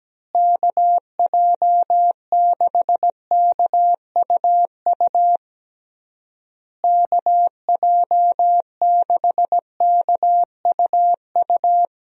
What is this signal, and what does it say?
Morse code, heard as a keyed tone, about 17 wpm, 700 Hz. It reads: KJ6KUU KJ6KUU
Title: USA Call Signs - 6 Characters - Repeated 3x - 17wpm